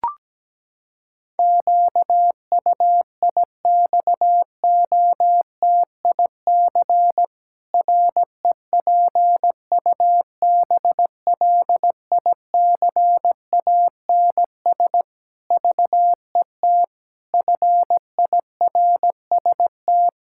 Morse code, heard as a keyed tone, about 17 wpm, 700 Hz. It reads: QUIXOTIC REPUBLICANS VET FIRST
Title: Pangram Challenge v2 - 17wpm